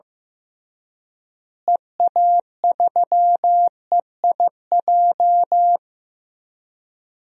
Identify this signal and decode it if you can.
Morse code, heard as a keyed tone, about 15 wpm, 700 Hz. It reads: EA3EIJ
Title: DXCC-100 Call Signs - 6 Characters - Repeated 3x - 15wpm